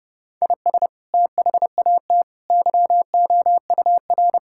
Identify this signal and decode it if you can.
Morse code, heard as a keyed tone, about 30 wpm, 700 Hz. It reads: IS THAT YOUR
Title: Sentences from Top 1000 Words - Review Part 5 - 30wpm